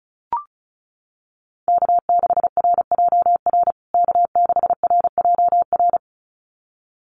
Morse code, heard as a keyed tone, about 34 wpm, 700 Hz. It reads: K6RJR K6RJR
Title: USA Call Signs Repeated 2x 35wpm